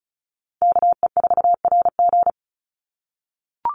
Morse code, heard as a keyed tone, about 36 wpm, 700 Hz. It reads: KE4RG
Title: Call Signs - 35wpm